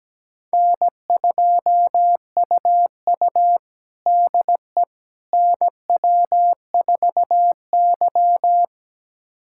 Morse code, heard as a keyed tone, about 17 wpm, 700 Hz. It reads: N2UU DE NW4Y
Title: USA Callsign DE Callsign - 4 Characters - 17wpm